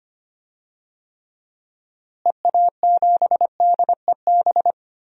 Morse code, heard as a keyed tone, about 25 wpm, 700 Hz. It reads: EA7DEB
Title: DXCC-100 Call Signs - 6 Characters - Repeated 2x - 25wpm